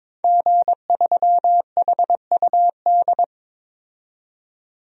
Morse code, heard as a keyed tone, about 22 wpm, 700 Hz. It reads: G3HUD
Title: DXCC-100 Call Signs - 5 Characters - 22wpm